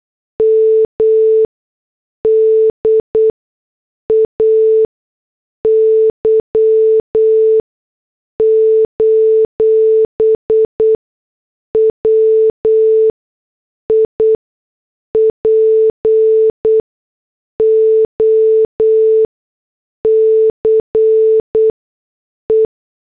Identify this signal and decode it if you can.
Morse code, heard as a keyed tone, about 8 wpm, spaced as for 6 wpm, 440 Hz. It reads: MDAY: WIPO CE